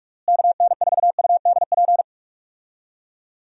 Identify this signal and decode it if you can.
Morse code, heard as a keyed tone, about 46 wpm, 700 Hz. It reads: KN4UDP
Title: USA Call Signs - 6 Characters - Repeated 3x - 45wpm